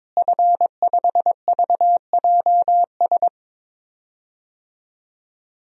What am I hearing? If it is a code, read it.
Morse code, heard as a keyed tone, about 22 wpm, 700 Hz. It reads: F5VJS
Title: DXCC-100 Call Signs - 5 Characters - Repeated 2x - 22wpm